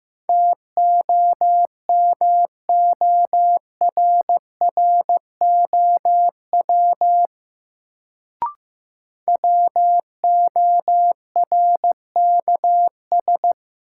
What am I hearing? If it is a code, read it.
Morse code, heard as a keyed tone, about 15 wpm, 700 Hz. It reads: TOMORROW WORKS
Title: Sentences from Top 1100 Words - New Word Drill Alt - 15wpm